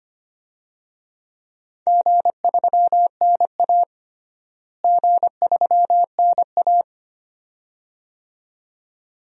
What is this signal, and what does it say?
Morse code, heard as a keyed tone, about 25 wpm, 700 Hz. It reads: G3NA G3NA
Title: DXCC-100 Call Signs - 4 Characters - Repeated 2x - 25wpm